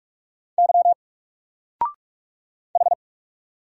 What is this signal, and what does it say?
Morse code, heard as a keyed tone, about 44 wpm, 700 Hz. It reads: Y H
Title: Single Character (Letter:Number:Punctuation:Prosign) 45wpm